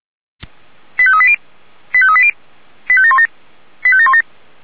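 Five-tone selective call (ZVEI), heard as quick runs of five tones, tones of about 70 ms. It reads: 85380 85380 85717 85717
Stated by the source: SDR-received ZVEI five-tone selective call, code 85746.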